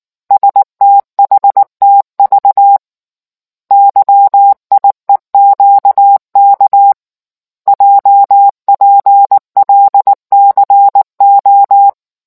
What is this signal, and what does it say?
Morse code, heard as a keyed tone, about 19 wpm, 800 Hz. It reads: STHTV YIEQK JPLCO